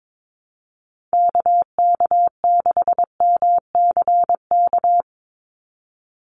Morse code, heard as a keyed tone, about 22 wpm, 700 Hz. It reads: KK6MCK
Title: USA Call Signs 22wpm